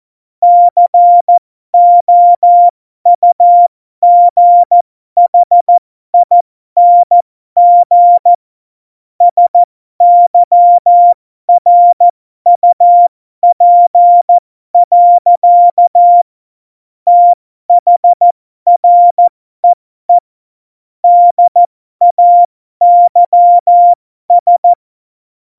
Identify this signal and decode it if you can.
Morse code, heard as a keyed tone, about 14 wpm, spaced as for 12 wpm, 700 Hz. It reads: COUGHING SYRUP. THREE DAYS
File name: Morsekurs / Witze